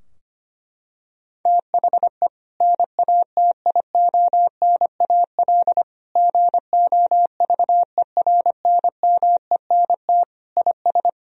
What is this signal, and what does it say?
Morse code, heard as a keyed tone, about 25 wpm, 700 Hz. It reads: THE NATIONAL GOVERNMENT IS
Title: Sentences from Top 1000 Words - Review Part 4 - 25wpm